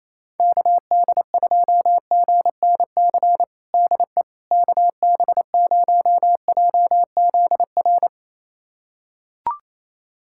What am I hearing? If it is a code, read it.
Morse code, heard as a keyed tone, about 28 wpm, 700 Hz. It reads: KD2GNC DE KB0JZR
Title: USA Callsign DE Callsign - 6 Characters - 28wpm